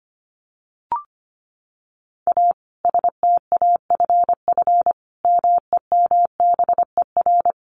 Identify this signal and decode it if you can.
Morse code, heard as a keyed tone, about 25 wpm, 700 Hz. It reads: A STAFF MEMBER
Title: Additional Easy Phrases 25wpm